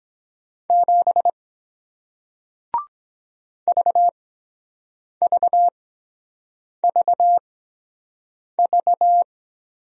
Morse code, heard as a keyed tone, about 22 wpm, spaced as for 7 wpm, 700 Hz. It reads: 7 VVVV